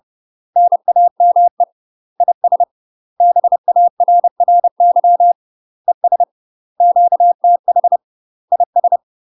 Morse code, heard as a keyed tone, about 30 wpm, 700 Hz. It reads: NAME IS BARRY ES QTH IS
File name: UK MASTER QSOs TRAINING / 30WPM / UK MASTER QSOs 0048